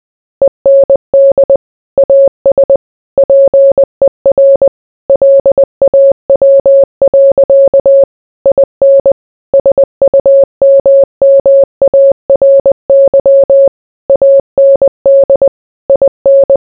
Morse code, heard as a keyed tone, about 20 wpm, 550 Hz. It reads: END AS PER LAW. IN SUMMARY AND IN